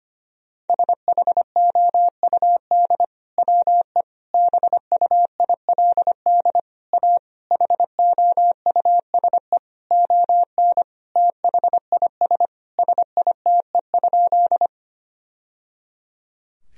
Morse code, heard as a keyed tone, about 25 wpm, 700 Hz. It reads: SHOUD WE BUILD A HOUSE ON THIS SITE?
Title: Sentences from Top 1000 Words - Review Part 4 - 25wpm